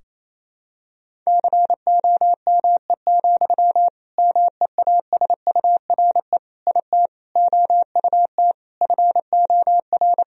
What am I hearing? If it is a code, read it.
Morse code, heard as a keyed tone, about 28 wpm, 700 Hz. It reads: COME, MEASURE IT OUT FOR
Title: Sentences from Top 1000 Words - Review Part 2 - 28wpm